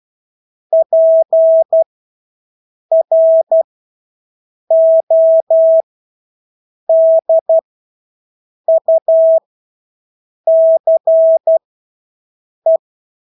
Morse code, heard as a keyed tone, about 12 wpm, spaced as for 6 wpm, 650 Hz. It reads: PRODUCE
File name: Engelske ord 12-6 wpm